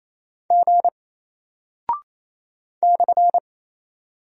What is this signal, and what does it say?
Morse code, heard as a keyed tone, about 28 wpm, 700 Hz. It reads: G /
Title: Single Character (Letter:Number:Punctuation:Prosign) Alt 28wpm